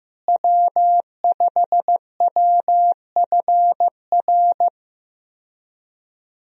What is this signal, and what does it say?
Morse code, heard as a keyed tone, about 15 wpm, 700 Hz. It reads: W5WFR
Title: USA Call Signs - 5 Characters - Repeated 3x - 15wpm